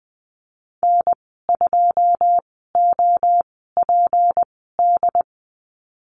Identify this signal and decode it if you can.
Morse code, heard as a keyed tone, about 20 wpm, spaced as for 14 wpm, 700 Hz. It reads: N2OPD
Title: 154 Lesson 25g - Letter p - Call Signs 20wpm